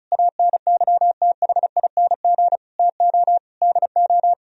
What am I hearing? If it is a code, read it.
Morse code, heard as a keyed tone, about 43 wpm, 700 Hz. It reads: ANYTHING TO DO
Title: Speed-Racing - Sets of 5 Words 35wpm